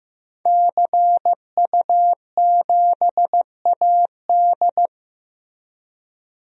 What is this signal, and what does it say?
Morse code, heard as a keyed tone, about 15 wpm, 700 Hz. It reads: CU7AD
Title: DXCC-100 Call Signs - 5 Characters - Repeated 3x - 15wpm